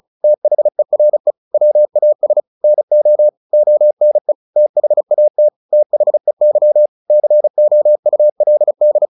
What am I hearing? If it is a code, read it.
Morse code, heard as a keyed tone, about 34 wpm, 600 Hz. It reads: THERE WAS NO ONE THAT THEY COULD
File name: the old man and the sea / the old man and the sea 1 35wpm